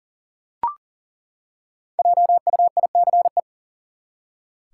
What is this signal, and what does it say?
Morse code, heard as a keyed tone, about 37 wpm, 700 Hz. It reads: JUICE
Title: Five Letter Words - 40wpm